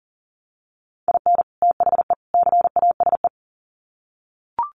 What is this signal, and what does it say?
Morse code, heard as a keyed tone, about 37 wpm, 700 Hz. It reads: IN THE CASE
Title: Sets of 3 Words 40wpm